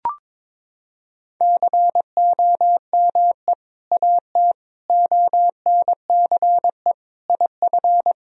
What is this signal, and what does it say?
Morse code, heard as a keyed tone, about 22 wpm, 700 Hz. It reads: COME AT ONCE IF
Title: Sentences from Top 1000 Words - Review Part 2 - 22wpm